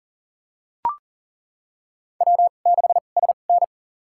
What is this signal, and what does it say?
Morse code, heard as a keyed tone, about 40 wpm, spaced as for 29 wpm, 700 Hz. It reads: W6SN
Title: USA Call Signs - 4 Characters - 2x Character Spacing - 40wpm